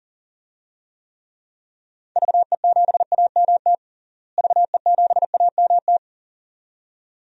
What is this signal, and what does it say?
Morse code, heard as a keyed tone, about 40 wpm, 700 Hz. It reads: VE7AMT VE7AMT
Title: DXCC-100 Call Signs - 6 Characters - Repeated 2x - 40wpm